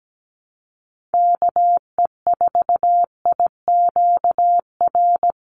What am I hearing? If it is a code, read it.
Morse code, heard as a keyed tone, about 17 wpm, 700 Hz. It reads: KE4IQR
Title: USA Call Signs 17wpm